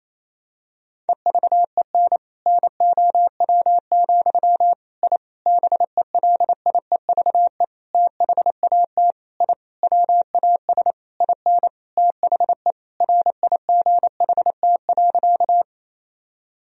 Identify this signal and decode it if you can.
Morse code, heard as a keyed tone, about 28 wpm, 700 Hz. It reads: EVEN NOW, I BELIEVE THAT I WAS IN THE RIGHT.
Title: Sentences from Top 1000 Words - Review Part 1 - 28wpm